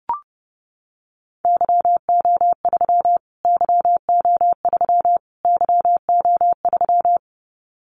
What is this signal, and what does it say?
Morse code, heard as a keyed tone, about 30 wpm, 690 Hz. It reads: YO3 YO3 YO3
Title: Callsign Prefixes for Top 25 DXCC - Repeated 3x - 30wpm